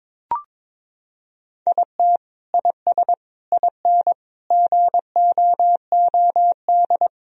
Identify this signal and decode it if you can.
Morse code, heard as a keyed tone, about 21 wpm, 700 Hz. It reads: IT IS IN GOOD
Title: Sentences from Top 1000 Words - Review Part 4 - 22wpm